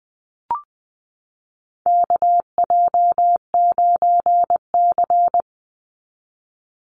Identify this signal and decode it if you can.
Morse code, heard as a keyed tone, about 20 wpm, 700 Hz. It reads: KJ9C
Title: Call Signs Repeated 3x - 20wpm